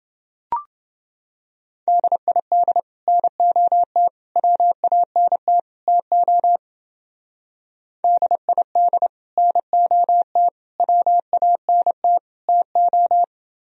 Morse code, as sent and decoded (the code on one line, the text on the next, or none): -.. .. -.. / -. --- - / .-- .- -. - / - --- / -.. .. -.. / -. --- - / .-- .- -. - / - ---
DID NOT WANT TO DID NOT WANT TO